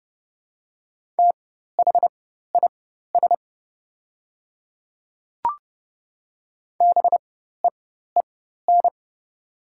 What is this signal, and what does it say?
Morse code, heard as a keyed tone, about 30 wpm, spaced as for 14 wpm, 690 Hz. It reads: THIS BEEN
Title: 4x Character Spacing - Top 100 Words - 30wpm